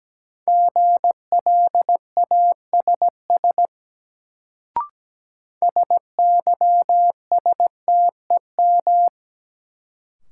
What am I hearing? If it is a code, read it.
Morse code, heard as a keyed tone, about 17 wpm, 700 Hz. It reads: GLASS SYSTEM
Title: Sentences from Top 1000 Words - Review Part 3 - New Word Drill - 17wpm